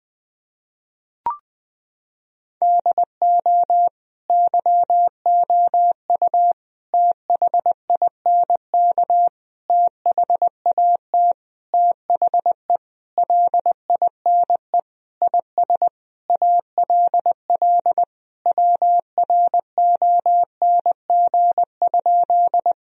Morse code, heard as a keyed tone, about 20 wpm, 700 Hz. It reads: DO YOU THINK THAT THE LINE IS ALL WRONG?
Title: Sentences from Top 1000 Words - Review Part 1 - 20wpm